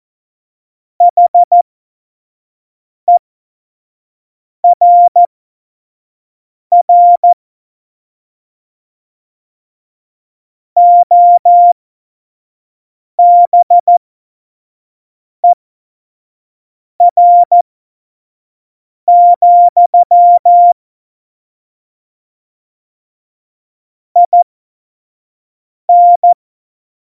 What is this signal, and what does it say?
Morse code, heard as a keyed tone, about 14 wpm, spaced as for 5 wpm, 700 Hz. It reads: HERR OBER, IN
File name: Morsekurs / Witze / Witz 03. September 2019, 25 F70